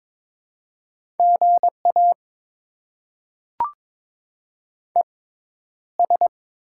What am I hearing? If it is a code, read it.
Morse code, heard as a keyed tone, about 22 wpm, spaced as for 11 wpm, 700 Hz. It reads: GA ES
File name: Top 20 QSO Elements - 6x Character Spacing - 22wpm